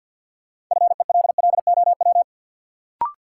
Morse code, heard as a keyed tone, about 53 wpm, 700 Hz. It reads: FELLOW
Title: Sets of 5 Words Encore - New Word Drill - 50wpm